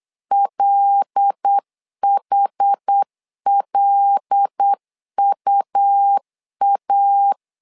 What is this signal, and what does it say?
Morse code, heard as a keyed tone, about 8 wpm, 800 Hz. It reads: LHLUA
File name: LEZIONI / 5 LBVC / 5 L 8wpm 2831